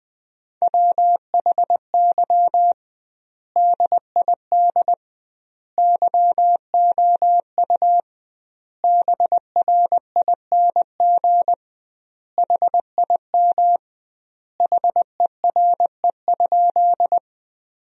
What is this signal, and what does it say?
Morse code, heard as a keyed tone, about 20 wpm, 700 Hz. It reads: WHY DID YOU BRING HIM HERE?
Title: 2x Word-Spacing - Sentences from Top 500 Words - 20wpm